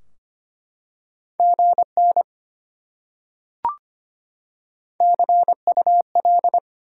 Morse code, heard as a keyed tone, about 25 wpm, 700 Hz. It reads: GN CUL
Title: Top 90 QSO Elements 25wpm